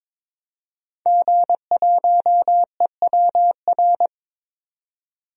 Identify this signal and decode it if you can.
Morse code, heard as a keyed tone, about 23 wpm, 700 Hz. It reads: G1EWR
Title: DXCC-100 Call Signs - 5 Characters - 22wpm